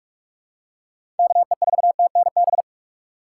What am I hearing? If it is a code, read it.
Morse code, heard as a keyed tone, about 43 wpm, 700 Hz. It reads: KE4TNB
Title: USA Call Signs - 2x Character Spacing - 45wpm